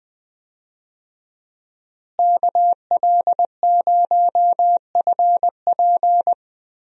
Morse code, heard as a keyed tone, about 20 wpm, 700 Hz. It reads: KL0FP
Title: USA Call Signs - 5 Characters - Repeated 2x - 20wpm